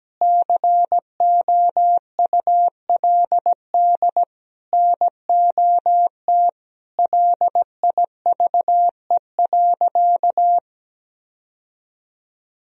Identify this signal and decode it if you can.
Morse code, heard as a keyed tone, about 17 wpm, 700 Hz. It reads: COULD NOT LIVE.
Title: Sentences from Top 1000 Words - Review Part 4 - 17wpm